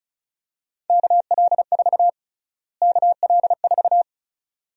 Morse code, heard as a keyed tone, about 34 wpm, 700 Hz. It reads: KL4 KL4
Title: USA Call Sign Prefixes - Repeated 2x - 35wpm